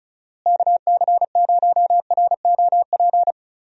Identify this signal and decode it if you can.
Morse code, heard as a keyed tone, about 34 wpm, 700 Hz. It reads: KC0ROP
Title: USA Call Signs - 6 Characters - Repeated 2x - 35wpm